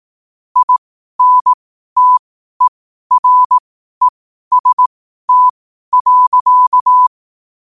Morse code, heard as a keyed tone, about 18 wpm, spaced as for 12 wpm, 1000 Hz. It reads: INTEREST.